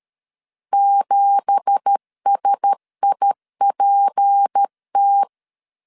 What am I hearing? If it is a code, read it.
Morse code, heard as a keyed tone, about 13 wpm, 800 Hz. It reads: 7SIPT
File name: LEZIONI / 9 2867 12wpm 2322